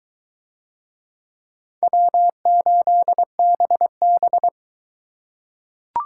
Morse code, heard as a keyed tone, about 22 wpm, 700 Hz. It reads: W8BB